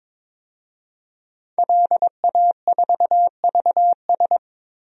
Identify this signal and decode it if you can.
Morse code, heard as a keyed tone, about 22 wpm, 700 Hz. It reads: LA4VS
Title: DXCC-100 Call Signs - 5 Characters - Repeated 2x - 22wpm